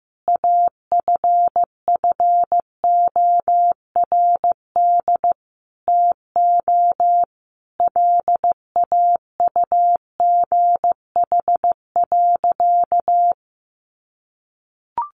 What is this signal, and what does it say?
Morse code, heard as a keyed tone, about 15 wpm, 700 Hz. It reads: AFFORD TO LAUGH.
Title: Sentences from Top 1100 Words 15wpm